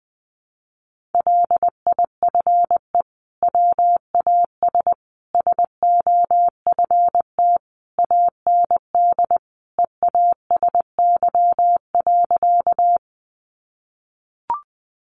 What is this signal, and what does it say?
Morse code, heard as a keyed tone, about 20 wpm, 700 Hz. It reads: LIFE WAS SOFT AND EASY.
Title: Sentences from Top 1500 Words 20wpm